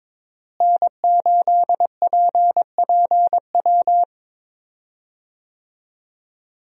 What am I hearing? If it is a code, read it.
Morse code, heard as a keyed tone, about 22 wpm, 700 Hz. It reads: N8PPW
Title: USA Call Signs - 5 Characters - Repeated 3x - 22wpm